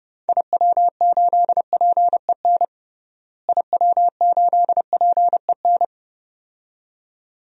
Morse code, heard as a keyed tone, about 30 wpm, 700 Hz. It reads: IW8PEN IW8PEN
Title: DXCC-100 Call Signs - 6 Characters - Repeated 2x - 30wpm